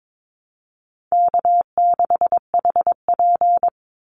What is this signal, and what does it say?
Morse code, heard as a keyed tone, about 22 wpm, 700 Hz. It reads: K6HP
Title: Easy Call Signs - 22wpm